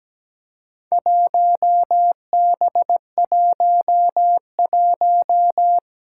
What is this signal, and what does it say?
Morse code, heard as a keyed tone, about 17 wpm, 700 Hz. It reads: 1B11
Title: Four Challenging Letter-Number Combinations 17wpm